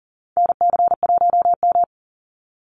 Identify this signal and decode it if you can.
Morse code, heard as a keyed tone, about 40 wpm, 700 Hz. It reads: NC1M
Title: Call Signs Repeated 2x - 40wpm